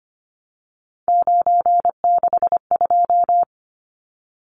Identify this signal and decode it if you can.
Morse code, heard as a keyed tone, about 25 wpm, 690 Hz. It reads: 962
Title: Numbers - 3-Digit 25wpm